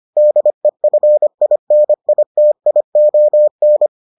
Morse code, heard as a keyed tone, about 25 wpm, 600 Hz. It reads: DEFINITION